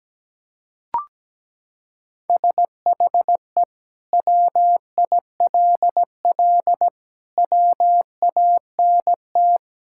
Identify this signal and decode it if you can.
Morse code, heard as a keyed tone, about 17 wpm, 700 Hz. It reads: SHE WILL WANT
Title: Sentences from Top 1600 Words 17wpm